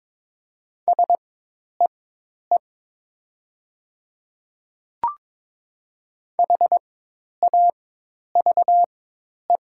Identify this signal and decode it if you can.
Morse code, heard as a keyed tone, about 21 wpm, spaced as for 10 wpm, 700 Hz. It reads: SEE HAVE
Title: 4x Character Spacing - Top 100 Words - 22wpm